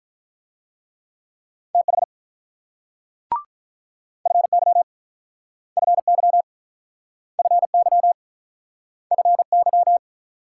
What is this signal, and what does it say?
Morse code, heard as a keyed tone, about 47 wpm, 700 Hz. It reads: TH FY FY FY FY